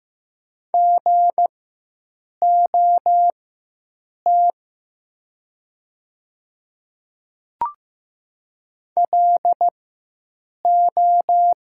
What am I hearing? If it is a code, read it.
Morse code, heard as a keyed tone, about 15 wpm, spaced as for 7 wpm, 700 Hz. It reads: GOT LO
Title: 4x Character Spacing - Top 300 Words - 15wpm